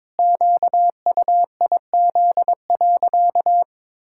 Morse code, heard as a keyed tone, about 22 wpm, 700 Hz. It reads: QUIZ.
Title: Pangram Challenge v2 - 22wpm